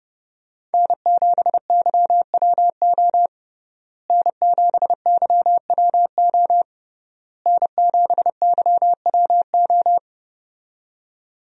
Morse code, heard as a keyed tone, about 30 wpm, 700 Hz. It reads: N7YWO N7YWO N7YWO
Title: USA Call Signs - 5 Characters - Repeated 3x - 30wpm